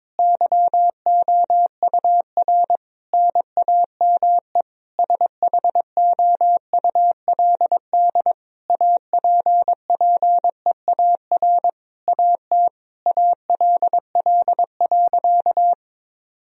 Morse code, heard as a keyed tone, about 22 wpm, 700 Hz. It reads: YOUR NAME SHOULD APPEAR AT ALL.
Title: Sentences from Top 1000 Words - Review Part 2 - 22wpm